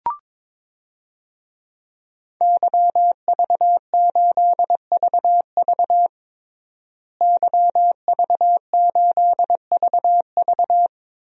Morse code, heard as a keyed tone, about 21 wpm, 700 Hz. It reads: YV8VV YV8VV
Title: DXCC-100 Call Signs - 5 Characters - Repeated 2x - 22wpm